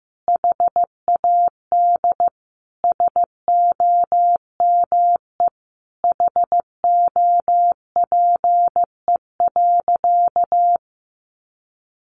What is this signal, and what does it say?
Morse code, heard as a keyed tone, about 15 wpm, 700 Hz. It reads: HAD SOME HOPE.
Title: Sentences from Top 1000 Words 15wpm